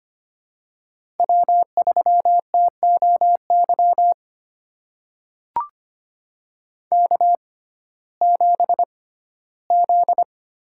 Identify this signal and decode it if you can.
Morse code, heard as a keyed tone, about 25 wpm, spaced as for 15 wpm, 700 Hz. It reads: W3TOY K7Z
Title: USA Call Signs - 5 Characters - 6x Character Spacing - 25wpm